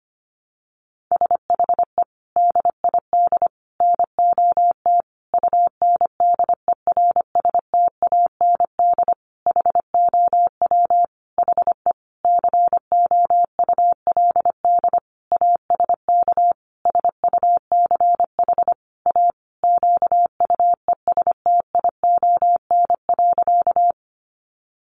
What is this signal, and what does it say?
Morse code, heard as a keyed tone, about 25 wpm, 700 Hz. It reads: SHE DID NOT UNDERSTAND HOW HE COULD ASK SUCH A QUESTION.
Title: Sentences from Top 600 Words 25wpm